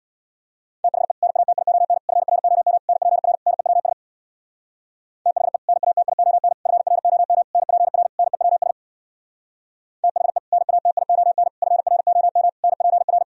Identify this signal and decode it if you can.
Morse code, heard as a keyed tone, about 62 wpm, 700 Hz. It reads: THE NATION FROM NPR NEWS THE NATION FROM NPR NEWS THE NATION FROM NPR
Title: Speed-Racing - Sets of 5 Words 50wpm